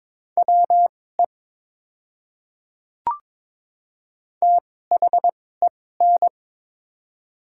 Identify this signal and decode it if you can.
Morse code, heard as a keyed tone, about 22 wpm, spaced as for 16 wpm, 700 Hz. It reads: WE THEN
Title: 2x Character Spacing - Top 100 Words - 22wpm